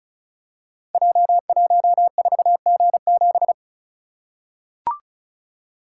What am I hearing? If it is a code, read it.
Morse code, heard as a keyed tone, about 36 wpm, 700 Hz. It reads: J14G7
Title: Code Groups - 5 Letter:Numbers - 35wpm